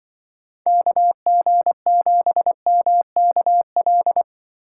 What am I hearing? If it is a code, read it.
Morse code, heard as a keyed tone, about 24 wpm, 700 Hz. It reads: KG7MKL
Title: Speed Racing - USA Call Signs 20wpm